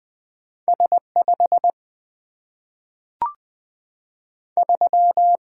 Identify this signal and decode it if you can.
Morse code, heard as a keyed tone, about 20 wpm, 700 Hz. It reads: S 5 3
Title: Single-Letter Single-Number 20wpm